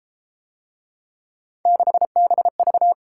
Morse code, heard as a keyed tone, about 31 wpm, 700 Hz. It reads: 6BV